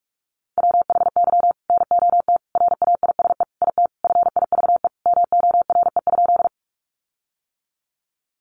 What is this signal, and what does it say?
Morse code, heard as a keyed tone, about 44 wpm, 700 Hz. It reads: WHY NOT RAISE IT FIVE MORE?
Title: Sentences from Top 1400 Words 45wpm